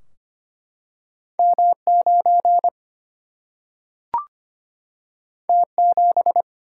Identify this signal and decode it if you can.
Morse code, heard as a keyed tone, about 25 wpm, 700 Hz. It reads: M 9 T 7
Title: Single-Letter Single-Number 25wpm